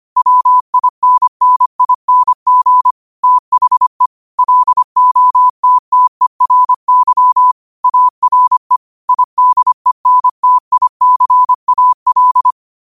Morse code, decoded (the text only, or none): WINNING THE LOTTERY ARE IDENTICAL